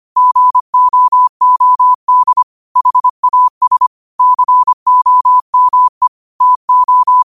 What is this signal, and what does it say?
Morse code, heard as a keyed tone, about 25 wpm, 1000 Hz. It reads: GOOD HAS COME TO